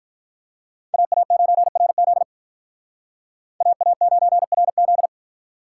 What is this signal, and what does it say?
Morse code, heard as a keyed tone, about 45 wpm, 700 Hz. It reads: AA9RZ AA9RZ